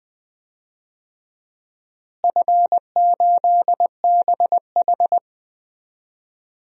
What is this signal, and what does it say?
Morse code, heard as a keyed tone, about 20 wpm, 700 Hz. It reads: F8BH
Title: DXCC-100 Call Signs - 4 Characters - Repeated 3x - 20wpm